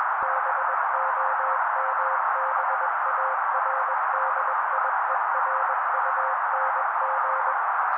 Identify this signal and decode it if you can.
Morse code, heard as a keyed tone, about 21 wpm, 530 Hz. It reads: BOMBARDIERUNG
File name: daily / 20wpm noise